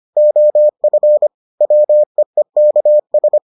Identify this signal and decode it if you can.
Morse code, heard as a keyed tone, about 25 wpm, 600 Hz. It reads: OF WEEKS